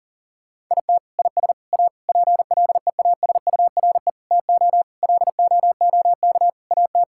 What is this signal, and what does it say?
Morse code, heard as a keyed tone, about 40 wpm, 700 Hz. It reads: IT IS A PLEASURE TO LOOK AT
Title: Sentences from Top 1600 Words 40wpm